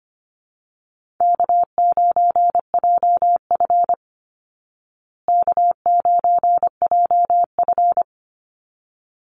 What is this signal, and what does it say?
Morse code, heard as a keyed tone, about 25 wpm, 700 Hz. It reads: K9JF K9JF
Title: Easy Call Signs Repeated 2x - 25wpm